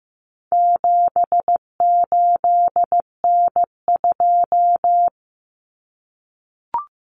Morse code, heard as a keyed tone, about 15 wpm, 700 Hz. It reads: 78N2
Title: Common Cut Numbers - 4-Digit 15wpm